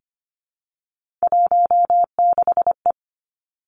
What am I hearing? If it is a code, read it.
Morse code, heard as a keyed tone, about 25 wpm, 700 Hz. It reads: 16E
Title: Three Challenging Letter-Number Combinations 25wpm